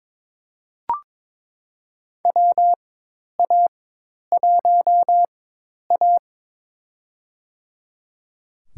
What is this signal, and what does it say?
Morse code, heard as a keyed tone, about 21 wpm, spaced as for 10 wpm, 700 Hz. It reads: WA1A
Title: USA Call Signs - 4 Characters - 4x Character Spacing - 22wpm